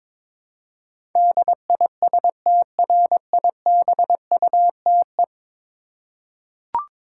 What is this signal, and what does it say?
Morse code, heard as a keyed tone, about 22 wpm, 700 Hz. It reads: DISTRIBUTE
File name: practice / 22wpm / 10 Letter Words - First 3 Letters Given - 22wpm